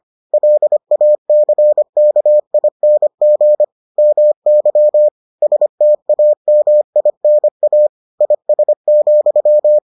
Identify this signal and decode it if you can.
Morse code, heard as a keyed tone, about 25 wpm, 600 Hz. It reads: LACKING MY STAMINA IS,